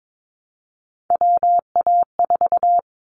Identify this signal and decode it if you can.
Morse code, heard as a keyed tone, about 22 wpm, 700 Hz. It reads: WA4
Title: Callsign Prefixes for Top 25 DXCC - Repeated 3x - 22wpm